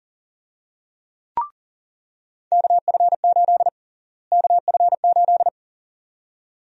Morse code, heard as a keyed tone, about 40 wpm, 700 Hz. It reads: KF8 KF8
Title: USA Call Sign Prefixes - Repeated 2x - 40wpm